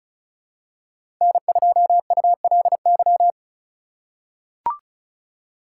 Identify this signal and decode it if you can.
Morse code, heard as a keyed tone, about 34 wpm, 700 Hz. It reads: N2ULY
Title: USA Call Signs - 5 Characters - 4x Character Spacing - 35wpm